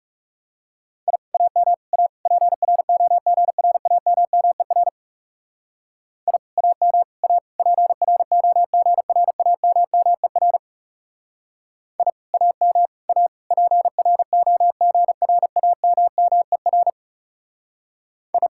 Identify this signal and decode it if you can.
Morse code, heard as a keyed tone, about 37 wpm, 700 Hz. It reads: I AM A PROGRAMMER I AM A PROGRAMMER I AM A PROGRAMMER I
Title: Speed-Racing Common QSO Phrases - 30wpm